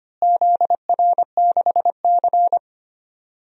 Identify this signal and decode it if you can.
Morse code, heard as a keyed tone, about 25 wpm, 700 Hz. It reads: ZR6C
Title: DXCC-100 Call Signs - 4 Characters - Repeated 3x - 25wpm